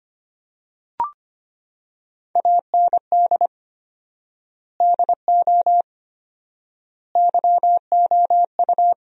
Morse code, heard as a keyed tone, about 25 wpm, 700 Hz. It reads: AND DO YOU
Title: 4x Word-Spacing - Sentences from Top 100 Words - 25wpm